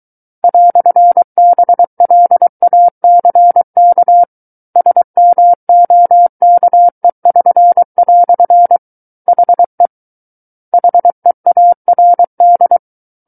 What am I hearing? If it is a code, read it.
Morse code, heard as a keyed tone, about 23 wpm, 700 Hz. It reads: "BLACK SMOKE<SN>" HE HEARD
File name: Book One / 14-Chapter 14 - 23 WPM